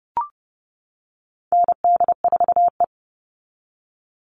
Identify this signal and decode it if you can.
Morse code, heard as a keyed tone, about 30 wpm, 700 Hz. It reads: ND4E
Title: All Cut Numbers - 4-Digit 30wpm